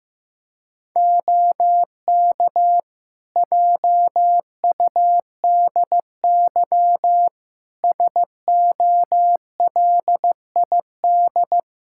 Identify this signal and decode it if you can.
Morse code, heard as a keyed tone, about 15 wpm, 700 Hz. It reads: OK JUDY SOLID
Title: Practice QSO - Exchange 3 - More Info - 15wpm